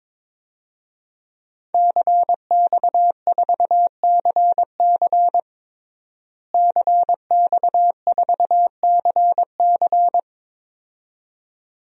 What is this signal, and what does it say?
Morse code, heard as a keyed tone, about 22 wpm, 700 Hz. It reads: CX4CC CX4CC
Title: DXCC-100 Call Signs - 5 Characters - Repeated 2x - 22wpm